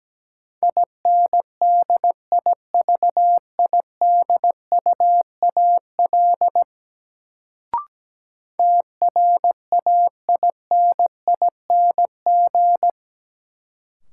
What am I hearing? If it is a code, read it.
Morse code, heard as a keyed tone, about 17 wpm, 700 Hz. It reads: INDIVIDUAL TRAINING
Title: Sentences from Top 1400 Words - New Word Drill Alt - 17wpm